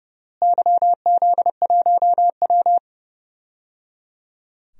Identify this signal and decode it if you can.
Morse code, heard as a keyed tone, about 30 wpm, 700 Hz. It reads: YZ1W
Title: DXCC-100 Call Signs - 4 Characters - Repeated 3x - 30wpm